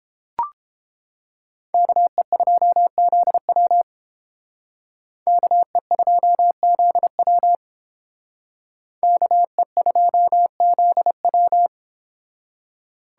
Morse code, heard as a keyed tone, about 30 wpm, 700 Hz. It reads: KE2ZW KE2ZW KE2ZW